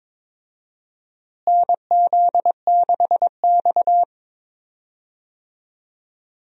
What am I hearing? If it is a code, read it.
Morse code, heard as a keyed tone, about 22 wpm, 700 Hz. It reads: NZ6X
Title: USA Call Signs - 4 Characters - Repeated 2x - 22wpm